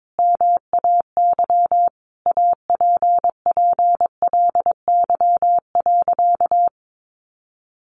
Morse code, heard as a keyed tone, about 22 wpm, 700 Hz. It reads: MAY APPLY.
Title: Sentences from Top 1400 Words 22wpm